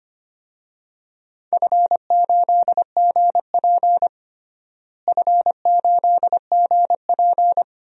Morse code, heard as a keyed tone, about 25 wpm, 700 Hz. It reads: F8GP F8GP
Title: DXCC-100 Call Signs - 4 Characters - Repeated 2x - 25wpm